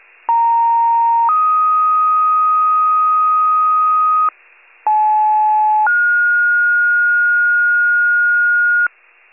Two-tone page, heard as a one-second tone, then a three-second tone, about 930 then 1250 Hz, about 860 then 1430 Hz, with noise. It is heard from 0.3 to 8.9 s.
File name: pager / 94 RESCUE OFC / wcalert